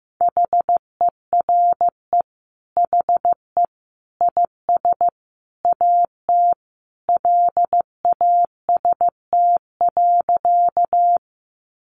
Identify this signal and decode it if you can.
Morse code, heard as a keyed tone, about 15 wpm, 700 Hz. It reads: HERE HE IS AT LAST.
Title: Sentences from Top 300 Words 15wpm